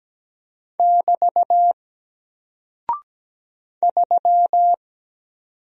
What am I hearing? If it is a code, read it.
Morse code, heard as a keyed tone, about 17 wpm, 700 Hz. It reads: <BT> 3
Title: Single Character (Letter:Number:Punctuation:Prosign) Alt 17wpm